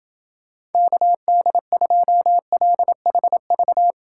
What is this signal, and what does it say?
Morse code, heard as a keyed tone, about 29 wpm, 700 Hz. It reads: KD2LHV